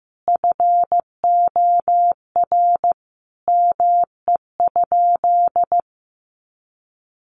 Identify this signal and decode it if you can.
Morse code, heard as a keyed tone, about 15 wpm, 700 Hz. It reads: FOR ME?
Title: Sentences from Top 1500 Words 15wpm